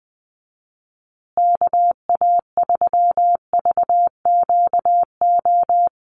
Morse code, heard as a keyed tone, about 20 wpm, 700 Hz. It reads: KA3VQO
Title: USA Call Signs Repeated 2x 20wpm